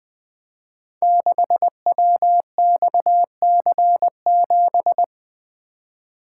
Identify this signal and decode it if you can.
Morse code, heard as a keyed tone, about 20 wpm, 700 Hz. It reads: 6WXC7
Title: Code Groups - 5 Letter:Numbers - 20wpm